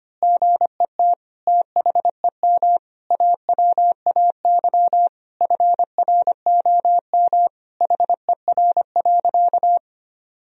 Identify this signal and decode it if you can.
Morse code, heard as a keyed tone, about 25 wpm, 700 Hz. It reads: GET THEM AWAY FROM HER.
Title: Sentences from Top 1000 Words - Review Part 2 - 25wpm